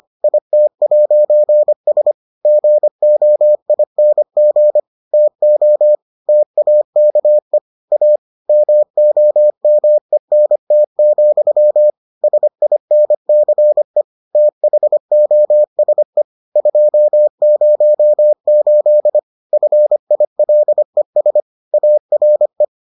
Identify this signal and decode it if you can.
Morse code, heard as a keyed tone, about 25 wpm, 600 Hz. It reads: IT'S GOING TO TAKE A MOMENT, SINCE THOSE 208 FILES ARE